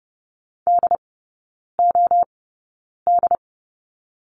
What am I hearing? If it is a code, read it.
Morse code, heard as a keyed tone, about 30 wpm, spaced as for 9 wpm, 700 Hz. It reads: DOD
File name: course / 30:10wpm / 087 Lesson 15e - Number 5 - Three Characters 30wpm